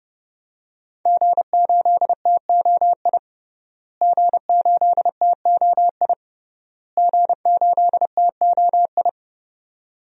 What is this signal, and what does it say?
Morse code, heard as a keyed tone, about 31 wpm, 700 Hz. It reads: G8TOI G8TOI G8TOI
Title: DXCC-100 Call Signs - 5 Characters - Repeated 3x - 30wpm